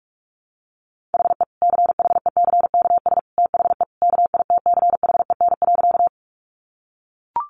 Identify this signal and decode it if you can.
Morse code, heard as a keyed tone, about 44 wpm, 700 Hz. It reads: HE CHECKS THE KITCHEN.
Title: Easy Phrases 45wpm